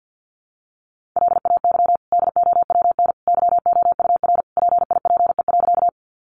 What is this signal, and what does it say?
Morse code, heard as a keyed tone, about 50 wpm, 700 Hz. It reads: LAY DOWN YOUR PIPE.
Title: Sentences from Top 1500 Words 50wpm